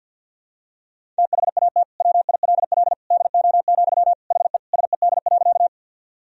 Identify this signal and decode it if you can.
Morse code, heard as a keyed tone, about 52 wpm, 700 Hz. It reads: THAT WILL DO, HE SED.
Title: Sentences from Top 100 Words and Abbreviations 50wpm